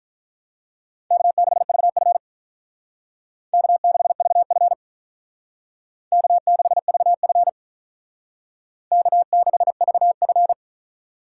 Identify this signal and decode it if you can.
Morse code, heard as a keyed tone, about 42 wpm, 700 Hz. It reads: K6VF K6VF K6VF K6VF
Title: Speed Racing - USA 1x2 Call Signs 35wpm